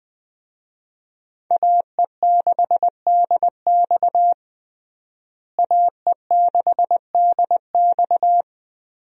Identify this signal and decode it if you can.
Morse code, heard as a keyed tone, about 20 wpm, 700 Hz. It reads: AE6DX AE6DX
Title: USA Call Signs - 5 Characters - Repeated 3x - 20wpm